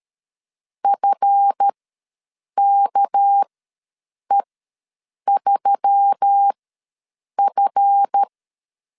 Morse code, heard as a keyed tone, about 13 wpm, spaced as for 7 wpm, 800 Hz. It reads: FKE3F
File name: LEZIONI / 7 GF9Z / 7 GF ws 12wpm 3140